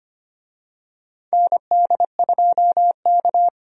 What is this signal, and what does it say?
Morse code, heard as a keyed tone, about 25 wpm, 700 Hz. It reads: ND2K
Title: USA Call Signs - 4 Characters - 25wpm